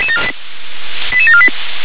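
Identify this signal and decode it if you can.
Five-tone selective call (ZVEI), heard as quick runs of five tones, tones of about 70 ms. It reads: 88748
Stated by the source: ZVEI five-tone selective call received on an SDR.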